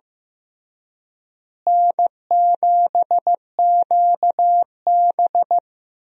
Morse code, heard as a keyed tone, about 15 wpm, 700 Hz. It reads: N7QB
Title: USA Call Signs - 4 Characters - Repeated 3x - 15wpm